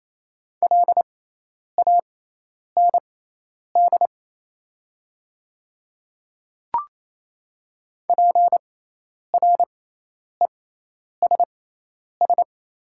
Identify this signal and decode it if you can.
Morse code, heard as a keyed tone, about 28 wpm, spaced as for 10 wpm, 700 Hz. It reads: LAND PRESS
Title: 6x Character Spacing - Top 300 Words - 28wpm